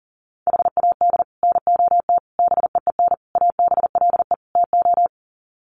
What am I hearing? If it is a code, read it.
Morse code, heard as a keyed tone, about 40 wpm, 700 Hz. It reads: HAD NOT BEEN ABLE TO
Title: Sets of 5 Words Encore 40wpm